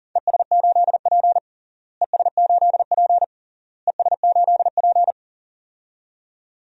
Morse code, heard as a keyed tone, about 37 wpm, 700 Hz. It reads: ES8P ES8P ES8P
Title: DXCC-100 Call Signs - 4 Characters - Repeated 3x - 40wpm